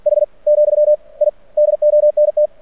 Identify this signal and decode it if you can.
Morse code, heard as a keyed tone, about 50 wpm, 600 Hz. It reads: H , I DONT
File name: archived / CRSnet